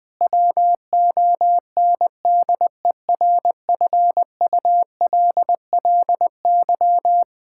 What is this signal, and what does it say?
Morse code, heard as a keyed tone, about 20 wpm, 700 Hz. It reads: WONDERFULLY
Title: Difficult Sentences with Hints - 20wpm